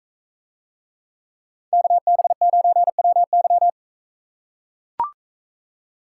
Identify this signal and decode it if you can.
Morse code, heard as a keyed tone, about 44 wpm, 700 Hz. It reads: KB9WY